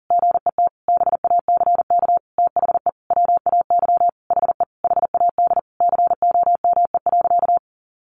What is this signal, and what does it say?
Morse code, heard as a keyed tone, about 40 wpm, 700 Hz. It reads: GET BACK THE WAY HE HAD COME.
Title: Sentences from Top 1100 Words 40wpm